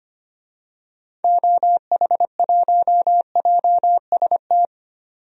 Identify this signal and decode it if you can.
Morse code, heard as a keyed tone, about 25 wpm, 700 Hz. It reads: OH1JST
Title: DXCC-100 Call Signs - 6 Characters - 25wpm